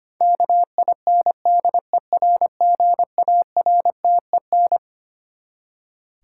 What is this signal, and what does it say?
Morse code, heard as a keyed tone, about 25 wpm, 700 Hz. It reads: KINDERGARTEN
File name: Twelve Letter Words - 25wpm